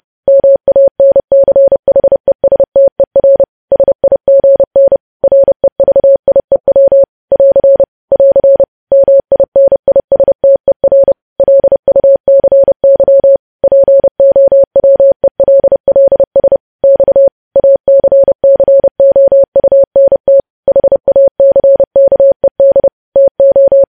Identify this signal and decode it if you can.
Morse code, heard as a keyed tone, about 30 wpm, 550 Hz. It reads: MANCHESTER SIGN REVIEW <AR> <AR> MINISTER LUCY POWELLS X ACCOUNT HACKED TO
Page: news headlines in morse code at 30 wpm